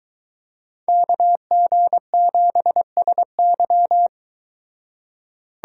Morse code, heard as a keyed tone, about 22 wpm, 700 Hz. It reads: KG7SY